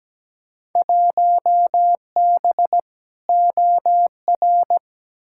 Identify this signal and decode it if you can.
Morse code, heard as a keyed tone, about 17 wpm, 700 Hz. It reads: 1B OR
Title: Field Day Exchanges - 17wpm